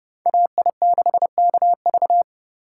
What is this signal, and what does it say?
Morse code, heard as a keyed tone, about 29 wpm, 700 Hz. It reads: AI6KV